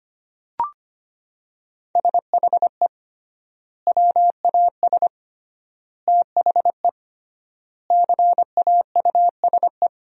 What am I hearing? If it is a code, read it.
Morse code, heard as a keyed tone, about 25 wpm, 700 Hz. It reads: SHE WAS THE CAUSE
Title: 3x Word-Spacing - Sentences from Top 200 Words - 25wpm